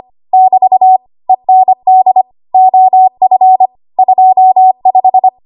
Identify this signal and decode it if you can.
Morse code, heard as a keyed tone, about 25 wpm, 750 Hz. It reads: <BT> END OF 25
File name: Archive / 25 WPM / 25WPM